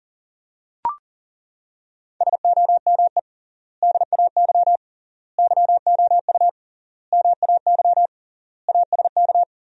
Morse code, heard as a keyed tone, about 40 wpm, 700 Hz. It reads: SOME DAY YOU MAY ASK
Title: 3x Word-Spacing - Sentences from Top 200 Words - 40wpm